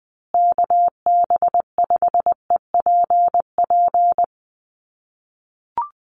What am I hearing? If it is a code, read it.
Morse code, heard as a keyed tone, about 20 wpm, 700 Hz. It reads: KB5EPP
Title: USA Call Signs Repeated 2x 20wpm